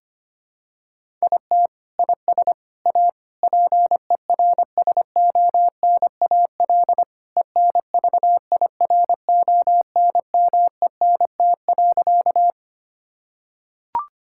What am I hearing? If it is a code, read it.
Morse code, heard as a keyed tone, about 25 wpm, 700 Hz. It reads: IT IS A PERSONAL ENVIRONMENT.
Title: Sentences from Top 1000 Words - Review Part 5 - 25wpm